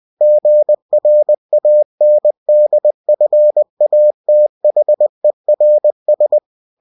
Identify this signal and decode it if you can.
Morse code, heard as a keyed tone, about 20 wpm, 600 Hz. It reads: GRANDFATHERS
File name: cw-books / winnie-the-pooh / winnie-20wpm-page-0022